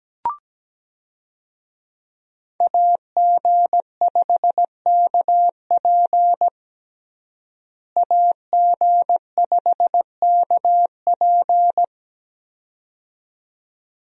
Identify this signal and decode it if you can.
Morse code, heard as a keyed tone, about 17 wpm, 700 Hz. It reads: AG5KP AG5KP
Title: USA Call Signs - 5 Characters - Repeated 2x - 17wpm